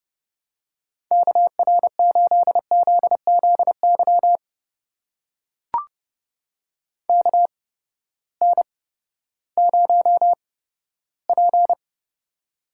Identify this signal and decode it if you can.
Morse code, heard as a keyed tone, about 32 wpm, spaced as for 15 wpm, 700 Hz. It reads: KR8ZZY KN0P
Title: USA Call Signs - 8x Character Spacing - 30wpm